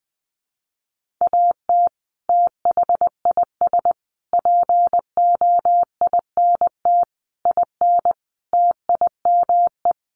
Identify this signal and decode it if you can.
Morse code, heard as a keyed tone, about 20 wpm, 700 Hz. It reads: AT THIS POINT IN TIME
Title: Sets of 5 Words Encore 20wpm